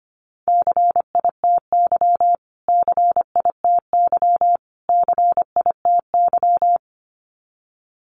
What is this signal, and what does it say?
Morse code, heard as a keyed tone, about 25 wpm, 700 Hz. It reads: CITY CITY CITY
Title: Top 300 Words Repeated 25wpm